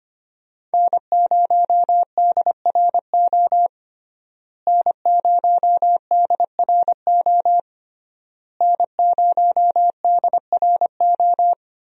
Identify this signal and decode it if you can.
Morse code, heard as a keyed tone, about 25 wpm, 700 Hz. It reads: N0DRO N0DRO N0DRO
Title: USA Call Signs - 5 Characters - Repeated 3x - 25wpm